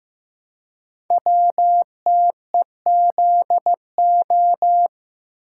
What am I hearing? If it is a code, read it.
Morse code, heard as a keyed tone, about 15 wpm, 700 Hz. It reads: WTEZO
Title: Code Groups - 5 Letters - 15wpm